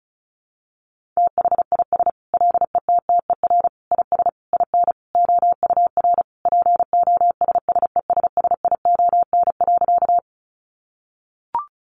Morse code, heard as a keyed tone, about 34 wpm, 700 Hz. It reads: THIS LETTER IS IN OUR POSSESSION.
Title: Sentences from Top 1400 Words 35wpm